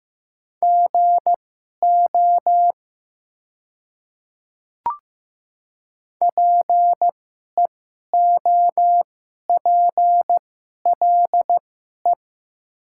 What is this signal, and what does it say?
Morse code, heard as a keyed tone, about 15 wpm, spaced as for 11 wpm, 690 Hz. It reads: GO PEOPLE
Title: 2x Character Spacing - Top 100 Words - 15wpm